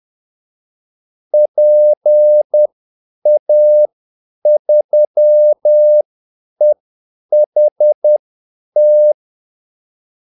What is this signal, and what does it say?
Morse code, heard as a keyed tone, about 10 wpm, spaced as for 8 wpm, 600 Hz. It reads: PA3EHT
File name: Callsigns-10-8wpm